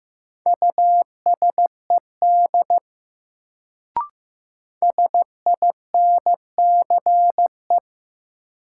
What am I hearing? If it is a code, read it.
Morse code, heard as a keyed tone, about 15 wpm, 700 Hz. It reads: USED SINCE
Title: Sets of 2 Words - New Word Drill - 15wpm